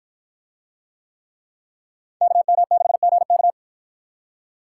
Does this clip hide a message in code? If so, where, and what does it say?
Yes: Morse code from 2.2 to 3.5 s, heard as a keyed tone, about 53 wpm, 700 Hz. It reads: KM6GK